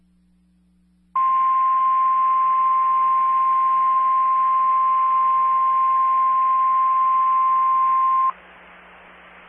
Emergency Alert System (SAME) tones (the 1050 Hz attention tone). It is heard from 1.1 to 8.3 s.
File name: RR Skywarn / SWIRA Linked Repeaters and SKYWARN Spotters